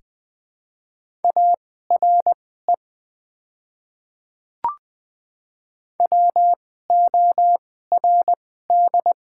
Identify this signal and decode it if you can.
Morse code, heard as a keyed tone, about 20 wpm, spaced as for 14 wpm, 700 Hz. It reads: ARE WORD
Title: 2x Character Spacing - Top 100 Words - 20wpm